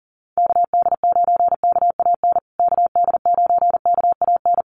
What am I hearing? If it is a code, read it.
Morse code, heard as a keyed tone, about 40 wpm, 700 Hz. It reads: KD9KAN KD9KAN
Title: USA Call Signs Repeated 3x 40wpm